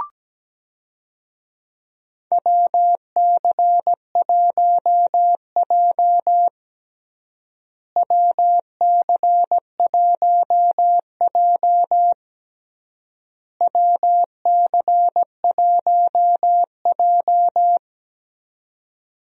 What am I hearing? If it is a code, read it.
Morse code, heard as a keyed tone, about 17 wpm, 700 Hz. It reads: WC1J WC1J WC1J
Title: USA Call Signs - 4 Characters - Repeated 3x - 17wpm